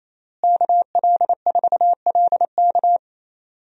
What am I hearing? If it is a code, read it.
Morse code, heard as a keyed tone, about 28 wpm, 700 Hz. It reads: KL4LK
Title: USA Call Signs - 5 Characters - Repeated 3x - 28wpm